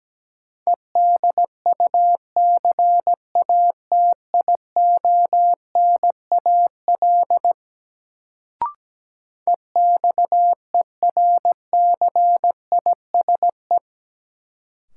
Morse code, heard as a keyed tone, about 17 wpm, 700 Hz. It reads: EDUCATIONAL EXERCISE
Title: Sentences from Top 1200 Words - New Word Drill Alt - 17wpm